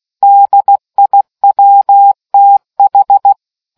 Morse code, heard as a keyed tone, about 16 wpm, 800 Hz. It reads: DIWTH